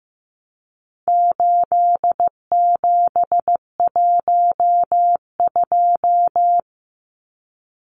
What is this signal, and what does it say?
Morse code, heard as a keyed tone, about 15 wpm, 700 Hz. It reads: 8712
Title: Numbers - 4-Digit 15wpm